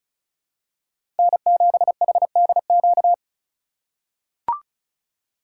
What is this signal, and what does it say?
Morse code, heard as a keyed tone, about 36 wpm, 700 Hz. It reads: N7HDQ
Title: USA Call Signs - 5 Characters - 3x Character Spacing - 35wpm